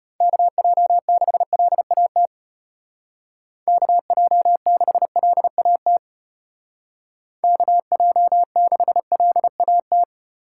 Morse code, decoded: KJ6LAT KJ6LAT KJ6LAT

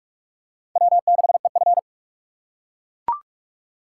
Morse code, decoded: W6EF

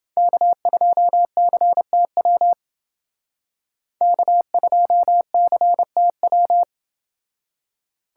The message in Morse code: K2CTW K2CTW

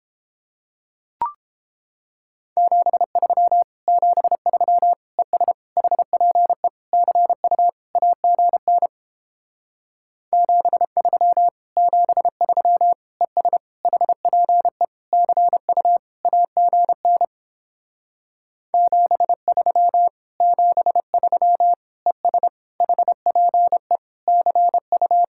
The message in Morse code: 73 73 ES HPE CU AGN 73 73 ES HPE CU AGN 73 73 ES HPE CU

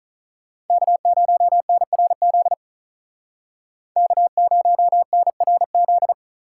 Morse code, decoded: K0NRZ K0NRZ

